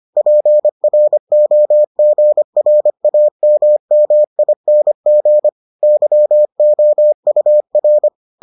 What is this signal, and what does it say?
Morse code, heard as a keyed tone, about 25 wpm, 600 Hz. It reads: PROGRAMMING YOUR